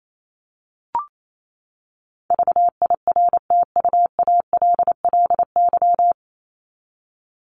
Morse code, heard as a keyed tone, about 28 wpm, 700 Hz. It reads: VIRTUALLY
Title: LY Suffix - Top 100 Words with Base Words 28wpm